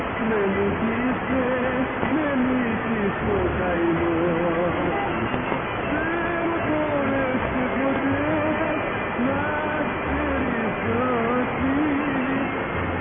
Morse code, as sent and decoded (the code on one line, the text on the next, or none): none